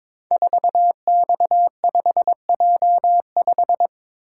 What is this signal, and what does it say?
Morse code, heard as a keyed tone, about 22 wpm, 700 Hz. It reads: 4X5J5
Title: Code Groups - 5 Letter:Numbers - 22wpm